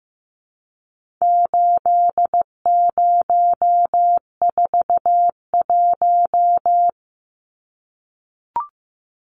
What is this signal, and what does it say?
Morse code, heard as a keyed tone, about 15 wpm, 700 Hz. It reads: 8041
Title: Numbers - 4-Digit 15wpm